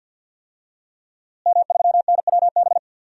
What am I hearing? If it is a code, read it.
Morse code, heard as a keyed tone, about 51 wpm, 700 Hz. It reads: M3NWB
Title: DXCC-100 Call Signs - 5 Characters - Repeated 3x - 50wpm